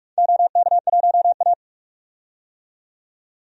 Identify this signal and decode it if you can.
Morse code, heard as a keyed tone, about 44 wpm, 700 Hz. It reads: OK1A
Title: DXCC-100 Call Signs - 4 Characters - 45wpm